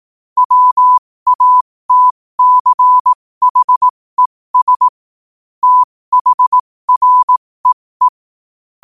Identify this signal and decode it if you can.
Morse code, heard as a keyed tone, about 18 wpm, spaced as for 15 wpm, 1000 Hz. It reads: WATCHES THREE
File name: qotdfiles / 15wpm